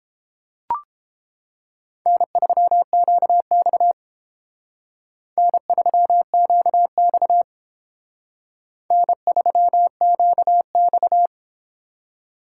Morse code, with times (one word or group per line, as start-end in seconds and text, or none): 2.1-3.9 N3QX
5.4-7.4 N3QX
8.9-11.3 N3QX